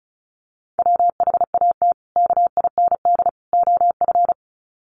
Morse code, heard as a keyed tone, about 34 wpm, 700 Hz. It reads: WHAT KIND OF
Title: Sets of 3 Words 35wpm